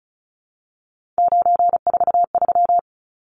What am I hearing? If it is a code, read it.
Morse code, heard as a keyed tone, about 34 wpm, 700 Hz. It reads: 943
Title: Numbers - 3-Digit 35wpm